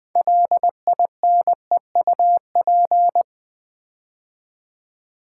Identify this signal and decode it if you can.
Morse code, heard as a keyed tone, about 20 wpm, 690 Hz. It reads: LINEUP
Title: Build A Word - 3 Letters At A Time - 6-Letter Words - 20wpm